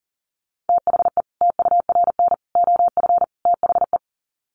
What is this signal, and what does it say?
Morse code, heard as a keyed tone, about 40 wpm, 700 Hz. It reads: THE TURN OF THE
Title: Sets of 4 Words 40wpm